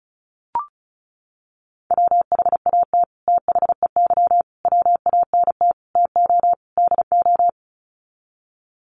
Morse code, heard as a keyed tone, about 34 wpm, 700 Hz. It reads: WHAT THEY WANT TO DO
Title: Sets of 5 Words Encore2 35wpm